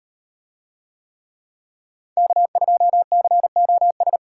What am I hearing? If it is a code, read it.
Morse code, heard as a keyed tone, about 38 wpm, 700 Hz. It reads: K2COS